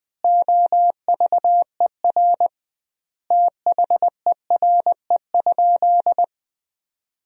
Morse code, heard as a keyed tone, about 21 wpm, 700 Hz. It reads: OVER THERE?
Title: 2x Word-Spacing - Sentences from Top 200 Words - 20wpm